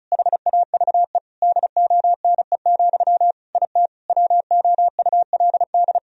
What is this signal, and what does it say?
Morse code, heard as a keyed tone, about 34 wpm, 700 Hz. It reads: HAVE DONE, IT WOULD
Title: Sentences from Top 1000 Words - Review Part 2 - 35wpm